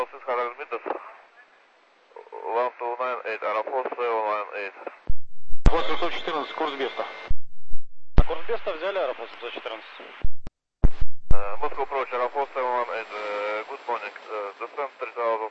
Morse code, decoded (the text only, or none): none